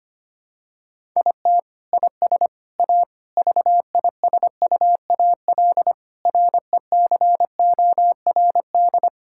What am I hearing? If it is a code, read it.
Morse code, heard as a keyed tone, about 25 wpm, 700 Hz. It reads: IT IS A VISUAL RECORD
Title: Sentences from Top 1000 Words - Review Part 6 - 25wpm